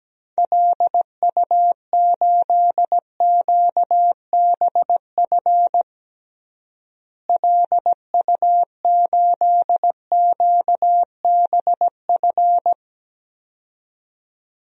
Morse code, heard as a keyed tone, about 17 wpm, 700 Hz. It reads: LU8QBF LU8QBF
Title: DXCC-100 Call Signs - 6 Characters - Repeated 2x - 17wpm